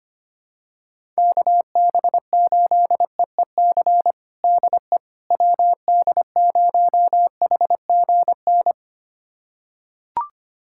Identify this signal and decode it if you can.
Morse code, heard as a keyed tone, about 25 wpm, 700 Hz. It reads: KB8EEC DE WD0HGN
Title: USA Callsign DE Callsign - 6 Characters - 25wpm